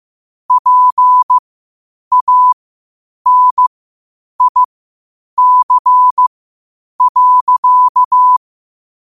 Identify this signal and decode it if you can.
Morse code, heard as a keyed tone, about 15 wpm, spaced as for 8 wpm, 1000 Hz. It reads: PANIC.
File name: qotdfiles / 8wpm